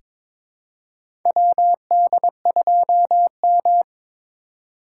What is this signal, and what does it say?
Morse code, heard as a keyed tone, about 22 wpm, 700 Hz. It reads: WD2M
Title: USA Call Signs - 4 Characters - 4x Character Spacing - 22wpm